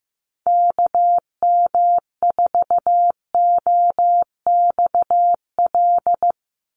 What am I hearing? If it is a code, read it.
Morse code, heard as a keyed tone, about 15 wpm, 700 Hz. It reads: KM4OXL
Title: USA Call Signs 15wpm